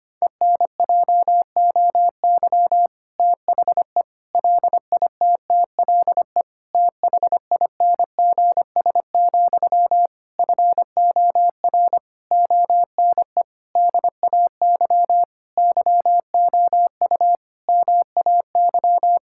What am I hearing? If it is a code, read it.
Morse code, heard as a keyed tone, about 25 wpm, 700 Hz. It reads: ENJOY THE LITTLE THINGS, FOR ONE DAY YOU MAY
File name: practice / 25wpm / Difficult Sentences with Hints - 25wpm